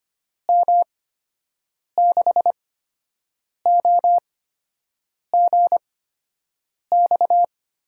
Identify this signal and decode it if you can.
Morse code, heard as a keyed tone, about 25 wpm, spaced as for 7 wpm, 700 Hz. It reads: M6OGX
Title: USA Call Signs - 6 Characters - 8x Character Spacing - 25wpm